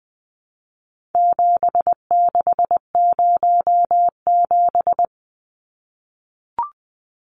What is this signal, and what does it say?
Morse code, heard as a keyed tone, about 20 wpm, 700 Hz. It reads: 7607
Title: Numbers - 4-Digit 20wpm